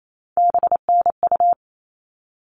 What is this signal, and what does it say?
Morse code, heard as a keyed tone, about 28 wpm, 700 Hz. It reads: BNU
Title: All Cut Numbers - 3-Digit 28wpm